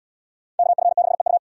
Morse code, heard as a keyed tone, about 77 wpm, 690 Hz. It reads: KF6EU